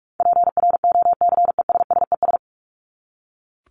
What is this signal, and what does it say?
Morse code, heard as a keyed tone, about 44 wpm, 700 Hz. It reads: PROCESSES
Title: ES:S Suffix - Top 100 Words with Base Words 45wpm